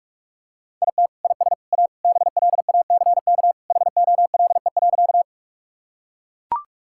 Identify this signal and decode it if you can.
Morse code, heard as a keyed tone, about 43 wpm, 700 Hz. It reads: IT IS A BLACK HOLE.
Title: Sentences from Top 1600 Words 45wpm